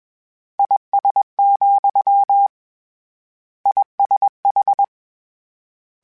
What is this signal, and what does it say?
Morse code, heard as a keyed tone, about 21 wpm, 800 Hz. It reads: IS, ISH
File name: General / General-mp3s / General @20